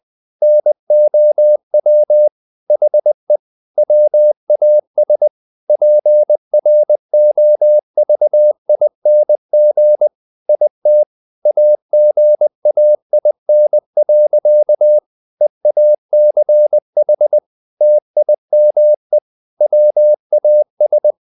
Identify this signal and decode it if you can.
Morse code, heard as a keyed tone, about 20 wpm, 600 Hz. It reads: NOW HE WAS PROVING IT AGAIN. EACH TIME WAS
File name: the old man and the sea / the old man and the sea part 1 20wpm